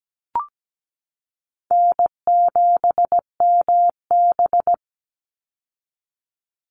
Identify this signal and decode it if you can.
Morse code, heard as a keyed tone, about 17 wpm, 700 Hz. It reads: N7MB
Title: Easy Call Signs Repeated 2x - 17wpm